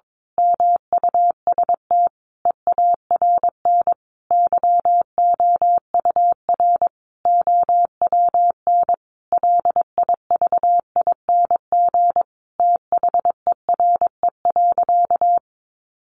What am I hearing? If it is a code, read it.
Morse code, heard as a keyed tone, about 22 wpm, 700 Hz. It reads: MUST EARN YOUR OWN LIVING THERE.
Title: Sentences from Top 1500 Words 22wpm